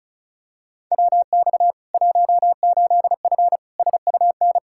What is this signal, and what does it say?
Morse code, heard as a keyed tone, about 34 wpm, 700 Hz. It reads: WX 18F SUN
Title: Practice QSO - Exchange 2 - Info - 35wpm